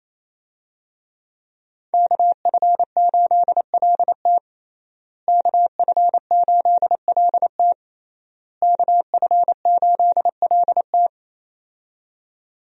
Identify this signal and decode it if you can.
Morse code, heard as a keyed tone, about 28 wpm, 700 Hz. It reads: KF8LT KF8LT KF8LT
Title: USA Call Signs - 5 Characters - Repeated 3x - 28wpm